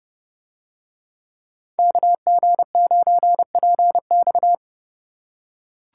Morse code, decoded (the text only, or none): KG9PX